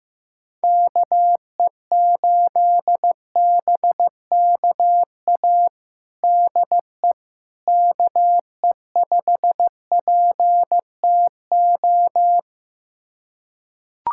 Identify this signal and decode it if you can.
Morse code, heard as a keyed tone, about 15 wpm, 700 Hz. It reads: KE8BKA DE KE5PTO
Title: USA Callsign DE Callsign - 6 Characters - 15wpm